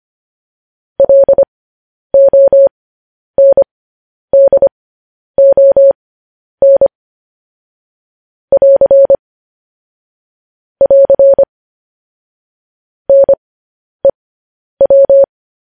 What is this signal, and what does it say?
Morse code, heard as a keyed tone, about 25 wpm, spaced as for 10 wpm, 550 Hz. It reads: LONDON <AR> <AR> NEW
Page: news headlines in morse code at 10 wpm